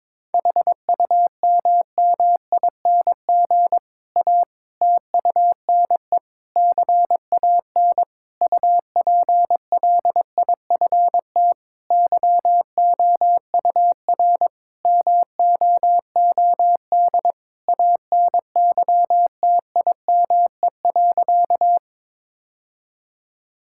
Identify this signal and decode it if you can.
Morse code, heard as a keyed tone, about 22 wpm, 700 Hz. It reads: HUMMING A TUNE CAN UPLIFT YOUR MOOD ANYTIME.
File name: Difficult Sentences with Hints - 22wpm